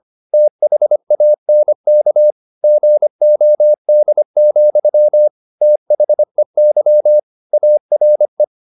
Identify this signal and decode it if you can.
Morse code, heard as a keyed tone, about 25 wpm, 600 Hz. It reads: THANK GOD, THEY ARE
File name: the old man and the sea part 1 25wpm